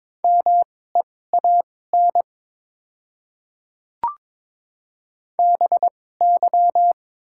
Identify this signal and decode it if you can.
Morse code, heard as a keyed tone, about 22 wpm, spaced as for 16 wpm, 700 Hz. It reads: MEAN BY
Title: 2x Character Spacing - Top 200 Words - 22wpm